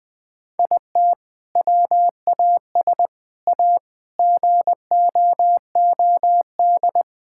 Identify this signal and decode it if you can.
Morse code, heard as a keyed tone, about 20 wpm, 700 Hz. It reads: IT WAS A GOOD
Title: Sentences from Top 1000 Words - Review Part 5 - 20wpm